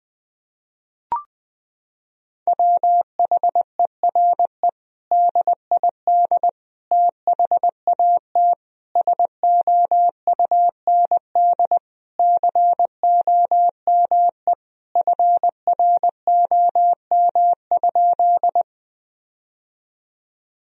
Morse code, heard as a keyed tone, about 20 wpm, 700 Hz. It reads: WHERE DID THAT SOUND COME FROM?
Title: Sentences from Top 1000 Words - Review Part 1 - 20wpm